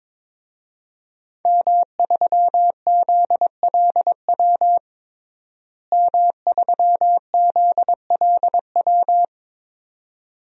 Morse code, heard as a keyed tone, about 22 wpm, 700 Hz. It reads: M3ZLW M3ZLW
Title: DXCC-100 Call Signs - 5 Characters - Repeated 2x - 22wpm